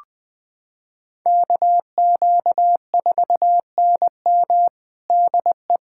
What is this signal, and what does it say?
Morse code, heard as a keyed tone, about 21 wpm, 700 Hz. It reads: KQ4NM DE
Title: USA Callsign DE Callsign 20wpm